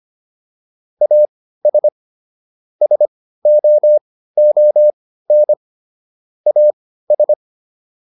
Morse code, heard as a keyed tone, about 25 wpm, spaced as for 15 wpm, 600 Hz. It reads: AS SOON AS